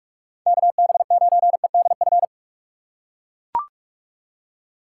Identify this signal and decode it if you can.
Morse code, heard as a keyed tone, about 41 wpm, 700 Hz. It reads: KB9EDF
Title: USA Call Signs - 6x Character Spacing - 45wpm